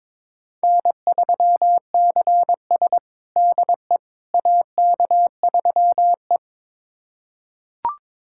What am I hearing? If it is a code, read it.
Morse code, heard as a keyed tone, about 21 wpm, 700 Hz. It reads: N3CS DE AK3E
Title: USA Callsign DE Callsign - 4 Characters - 22wpm